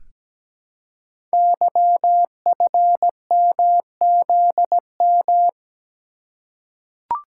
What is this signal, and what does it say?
Morse code, heard as a keyed tone, about 17 wpm, 700 Hz. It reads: YFMZM
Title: Code Groups - 5 Letters - 17wpm